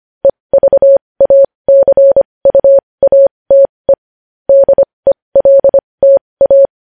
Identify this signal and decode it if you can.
Morse code, heard as a keyed tone, about 25 wpm, spaced as for 20 wpm, 550 Hz. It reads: EVACUATE DELTA
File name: podcast / News headlines-1x-25x20